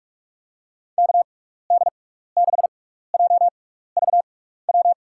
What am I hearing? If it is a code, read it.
Morse code, heard as a keyed tone, about 46 wpm, spaced as for 15 wpm, 700 Hz. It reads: KD6JVW
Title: USA Call Signs - 6x Character Spacing - 45wpm